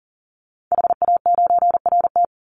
Morse code, heard as a keyed tone, about 40 wpm, 700 Hz. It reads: HA9RT
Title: Call Signs Repeated 2x - 40wpm